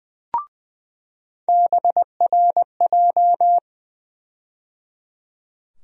Morse code, heard as a keyed tone, about 20 wpm, 700 Hz. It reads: BRJ
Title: USA Call Sign Postfixes - 3 Letters 20wpm